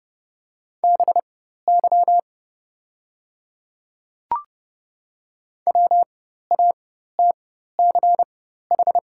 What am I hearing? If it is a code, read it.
Morse code, heard as a keyed tone, about 30 wpm, spaced as for 14 wpm, 700 Hz. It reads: BY WATCH
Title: 4x Character Spacing - Top 300 Words - 30wpm